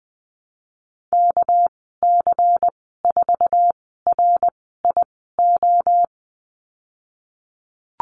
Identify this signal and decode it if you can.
Morse code, heard as a keyed tone, about 20 wpm, spaced as for 15 wpm, 700 Hz. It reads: KC4RIO
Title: 219 Lesson 35g - Number 0 - Call Signs 20wpm